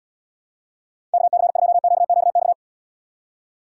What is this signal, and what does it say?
Morse code, heard as a keyed tone, about 76 wpm, 700 Hz. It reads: KC2QQX